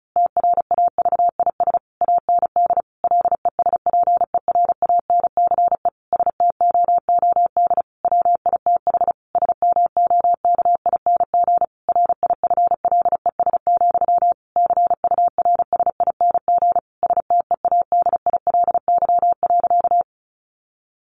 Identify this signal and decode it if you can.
Morse code, heard as a keyed tone, about 34 wpm, 700 Hz. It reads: TRAVIS AND LESPERANCE STOOD WITH SMOKING RIFLES, CURSING STEADILY.
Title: A Sound Of Thunder 35wpm